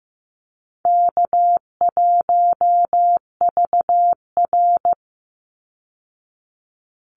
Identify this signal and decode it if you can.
Morse code, heard as a keyed tone, about 15 wpm, 700 Hz. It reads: K1VR
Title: Easy Call Signs Repeated 3x - 15wpm